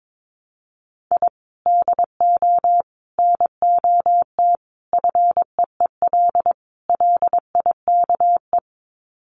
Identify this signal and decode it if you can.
Morse code, heard as a keyed tone, about 22 wpm, 700 Hz. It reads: I DO NOT FEEL LIKE
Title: Sets of 5 Words Encore 22wpm